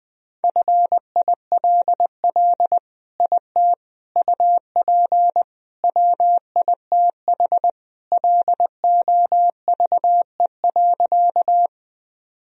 Morse code, heard as a keyed tone, about 21 wpm, 700 Hz. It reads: FILL IT UP WITH LOVE.
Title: Sentences from Top 1000 Words - Review Part 3 - 20wpm